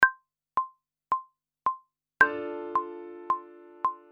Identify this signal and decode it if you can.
There is no code.